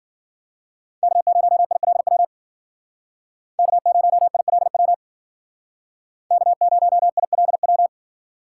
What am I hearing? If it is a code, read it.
Morse code, heard as a keyed tone, about 52 wpm, 700 Hz. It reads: K0ILW K0ILW K0ILW